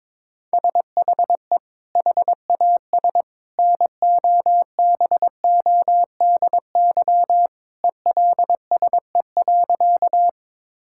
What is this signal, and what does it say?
Morse code, heard as a keyed tone, about 22 wpm, 700 Hz. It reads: SHE HAS NOBODY ELSE.
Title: Sentences from Top 1600 Words 22wpm